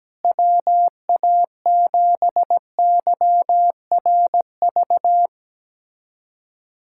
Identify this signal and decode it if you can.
Morse code, heard as a keyed tone, about 17 wpm, 700 Hz. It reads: WA7YRV
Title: USA Callsign DE Callsign 17wpm